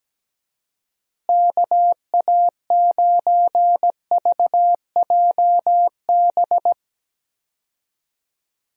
Morse code, heard as a keyed tone, about 17 wpm, 700 Hz. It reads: KA9VJB
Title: USA Call Signs - 6 Characters - Repeated 2x - 17wpm